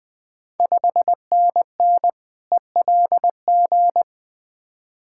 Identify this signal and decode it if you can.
Morse code, heard as a keyed tone, about 20 wpm, 690 Hz. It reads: 5NN ELG
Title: Ontario QSO Party Exchange 20wpm